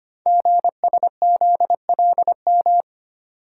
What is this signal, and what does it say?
Morse code, heard as a keyed tone, about 25 wpm, 700 Hz. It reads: GSZLM
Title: Code Groups - 5 Letters - 25wpm